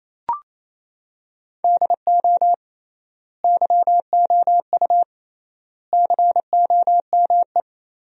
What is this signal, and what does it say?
Morse code, heard as a keyed tone, about 28 wpm, 700 Hz. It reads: DO YOU COME
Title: 3x Word-Spacing - Sentences from Top 200 Words - 28wpm